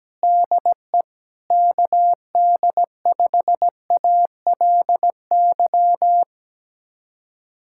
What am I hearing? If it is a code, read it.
Morse code, heard as a keyed tone, about 17 wpm, 700 Hz. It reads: DE KD5ALY
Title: USA Callsign DE Callsign 17wpm